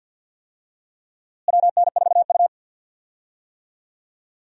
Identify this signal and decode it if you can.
Morse code, heard as a keyed tone, about 51 wpm, 700 Hz. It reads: WN4U
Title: USA Call Signs - 4 Characters - Repeated 2x - 50wpm